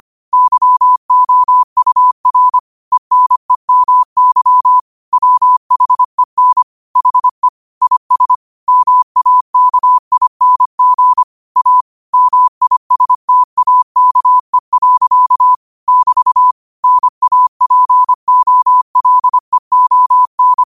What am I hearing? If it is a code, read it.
Morse code, heard as a keyed tone, about 25 wpm, 1000 Hz. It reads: YOUR ENEMY WHEN HE IS MAKING A MISTAKE. <BT> NAPOLEON
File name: qotdfiles / 25wpm